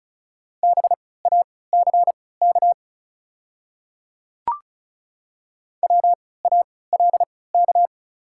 Morse code, heard as a keyed tone, about 36 wpm, spaced as for 20 wpm, 700 Hz. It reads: BACK WALK
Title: 3x Character Spacing - Top 300 Words - 35wpm